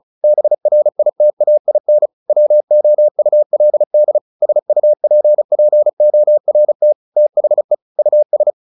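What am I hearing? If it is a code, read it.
Morse code, heard as a keyed tone, about 36 wpm, 600 Hz. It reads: BRITAIN WOULD SUPPORT THE US
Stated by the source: Morse practice at 35 wpm from CW med Gnister, sending Ken Follett's novel Never in English.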